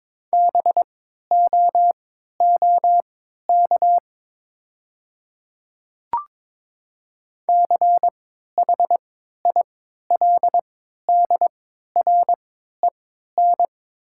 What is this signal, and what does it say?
Morse code, heard as a keyed tone, about 22 wpm, spaced as for 12 wpm, 700 Hz. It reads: BOOK CHILDREN
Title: 3x Character Spacing - Top 300 Words - 22wpm